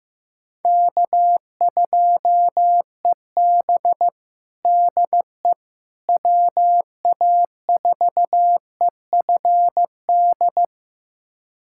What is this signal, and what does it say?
Morse code, heard as a keyed tone, about 15 wpm, 700 Hz. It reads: K2EB DE WA4EFD
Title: USA Callsign DE Callsign 15wpm